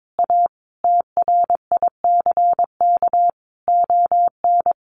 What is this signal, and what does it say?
Morse code, heard as a keyed tone, about 22 wpm, 700 Hz. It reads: A TRICK ON
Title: Sentences from Top 1500 Words 22wpm